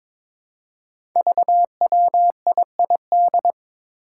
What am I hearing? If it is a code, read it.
Morse code, heard as a keyed tone, about 22 wpm, 700 Hz. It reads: VWIID
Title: Five Challenging Letter-Number Combinations 22wpm